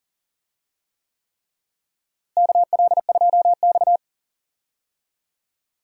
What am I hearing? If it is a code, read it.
Morse code, heard as a keyed tone, about 37 wpm, 700 Hz. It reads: KL2X
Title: USA Call Signs - 4 Characters - Repeated 3x - 40wpm